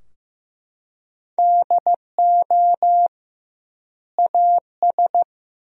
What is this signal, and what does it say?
Morse code, heard as a keyed tone, about 16 wpm, 700 Hz. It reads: DO AS
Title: 2x Word-Spacing - Sentences from Top 100 Words - 15wpm